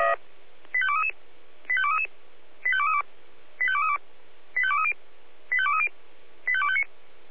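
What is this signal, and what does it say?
Five-tone selective call (ZVEI), heard as quick runs of five tones, tones of about 70 ms. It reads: 85233 85233 85232 85232 85230 85230 85259
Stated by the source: ZVEI five-tone selective call 85885 received by SDR.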